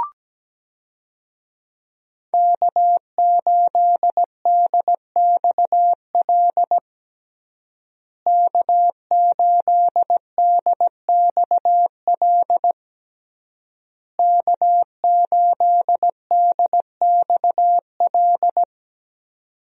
Morse code, heard as a keyed tone, about 17 wpm, 700 Hz. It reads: K8DXL K8DXL K8DXL
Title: USA Call Signs - 5 Characters - Repeated 3x - 17wpm